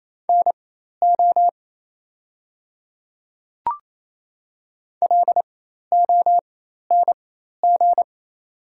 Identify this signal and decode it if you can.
Morse code, heard as a keyed tone, about 28 wpm, spaced as for 13 wpm, 700 Hz. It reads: NO LONG
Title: 4x Character Spacing - Top 100 Words - 28wpm